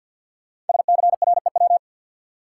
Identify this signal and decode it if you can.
Morse code, heard as a keyed tone, about 53 wpm, 700 Hz. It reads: SCREW
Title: Sentences from Top 1600 Words - New Word Drill Alt - 50wpm